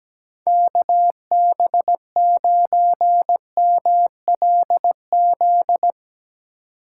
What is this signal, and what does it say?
Morse code, heard as a keyed tone, about 17 wpm, 700 Hz. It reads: KB9MLZ
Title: USA Call Signs - 6 Characters - 2x Character Spacing - 17wpm